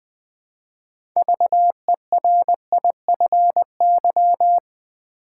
Morse code, heard as a keyed tone, about 20 wpm, 700 Hz. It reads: VERIFY
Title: 6 Letter Words - First 3 Letters Given - 20wpm